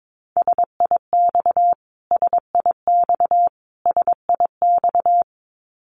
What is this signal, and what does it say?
Morse code, heard as a keyed tone, about 22 wpm, 700 Hz. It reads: SIX SIX SIX
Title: Top 500 Words Repeated 22wpm